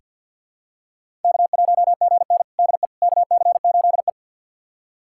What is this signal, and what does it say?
Morse code, heard as a keyed tone, about 49 wpm, 700 Hz. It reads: K1GN DE KC8E
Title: USA Callsign DE Callsign - 4 Characters - 50wpm